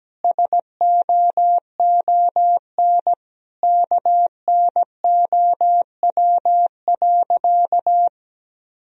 Morse code, heard as a keyed tone, about 17 wpm, 700 Hz. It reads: SOON KNOW.
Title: Sentences from Top 400 Words and Abbreviations 17wpm